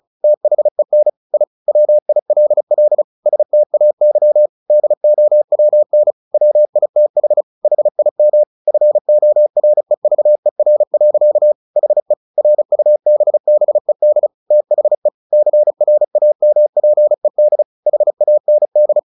Morse code, heard as a keyed tone, about 35 wpm, 600 Hz. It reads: THEN I WILL STAY DOWN WITH HIM FOREVER. HE RUBBED THE CRAMPED HAND
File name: the old man and the sea / the old man and the sea 1 35wpm